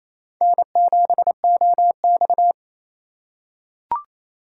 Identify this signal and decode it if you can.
Morse code, heard as a keyed tone, about 28 wpm, 700 Hz. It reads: N7OX
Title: USA Call Signs - 4x Character Spacing - 28wpm